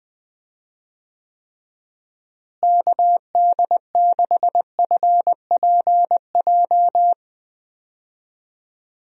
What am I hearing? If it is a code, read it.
Morse code, heard as a keyed tone, about 20 wpm, 700 Hz. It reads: KD6FPJ
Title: USA Call Signs - 6 Characters - Repeated 3x - 20wpm